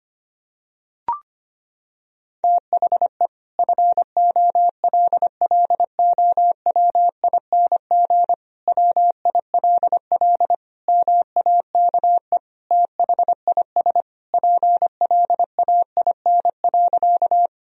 Morse code, heard as a keyed tone, about 25 wpm, 700 Hz. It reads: THE FOLLOWING WILL MAKE THIS PLAIN.
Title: Sentences from Top 1000 Words - Review Part 4 - 25wpm